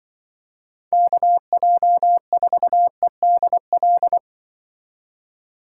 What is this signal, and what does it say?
Morse code, heard as a keyed tone, about 25 wpm, 700 Hz. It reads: KJ4EDL
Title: Speed Racing - USA 2x3 Call Signs 20wpm